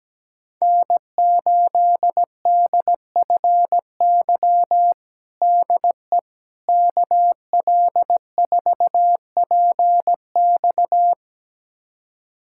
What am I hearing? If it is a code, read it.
Morse code, heard as a keyed tone, about 17 wpm, 700 Hz. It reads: N8DFY DE KL4PX
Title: USA Callsign DE Callsign - 5 Characters - 17wpm